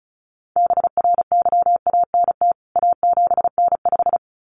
Morse code, heard as a keyed tone, about 34 wpm, 700 Hz. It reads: BRYANT A7N5
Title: CWT Exchange 35wpm